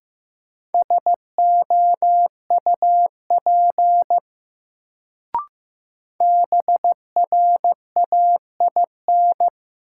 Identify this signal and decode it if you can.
Morse code, heard as a keyed tone, about 15 wpm, 700 Hz. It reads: SOUP BRAIN
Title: Sentences from Top 1000 Words - Review Part 6 - New Word Drill - 15wpm